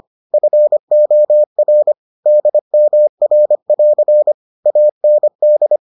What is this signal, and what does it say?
Morse code, heard as a keyed tone, about 25 wpm, 600 Hz. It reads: FOR DMR<AR> AND